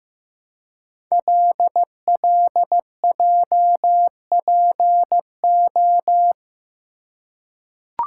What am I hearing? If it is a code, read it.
Morse code, heard as a keyed tone, about 15 wpm, 700 Hz. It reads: LLJPO
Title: Code Groups - 5 Letters - 15wpm